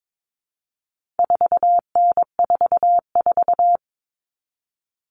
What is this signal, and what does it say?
Morse code, heard as a keyed tone, about 22 wpm, 700 Hz. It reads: 4N44
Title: All Cut Numbers - 4-Digit 22wpm